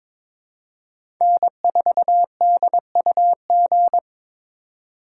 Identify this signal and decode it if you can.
Morse code, heard as a keyed tone, about 22 wpm, 700 Hz. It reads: N4DUG
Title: USA Call Signs - 5 Characters - 4x Character Spacing - 22wpm